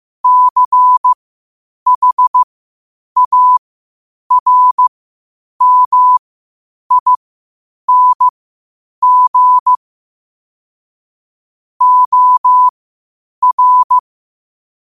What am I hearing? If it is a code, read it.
Morse code, heard as a keyed tone, about 15 wpm, spaced as for 8 wpm, 1000 Hz. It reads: CHARMING OR
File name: qotdfiles / 8wpm